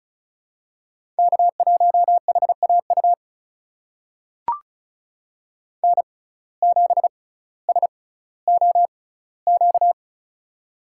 Morse code, heard as a keyed tone, about 36 wpm, spaced as for 18 wpm, 700 Hz. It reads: K1HAU N7SOQ
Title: USA Call Signs - 5 Characters - 6x Character Spacing - 35wpm